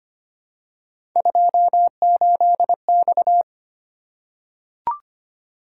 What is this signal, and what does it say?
Morse code, heard as a keyed tone, about 25 wpm, 700 Hz. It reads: 28X
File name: Three Letter-Number Combinations - 25wpm